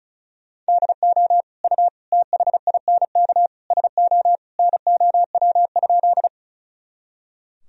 Morse code, heard as a keyed tone, about 36 wpm, 700 Hz. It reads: DO U THINK SO NOW?
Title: Sentences from Top 200 Words and Abbreviations 35wpm